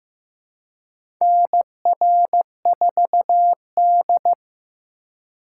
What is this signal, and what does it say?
Morse code, heard as a keyed tone, about 15 wpm, 700 Hz. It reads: NR4D
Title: USA Call Signs - 4 Characters - 2x Character Spacing - 15wpm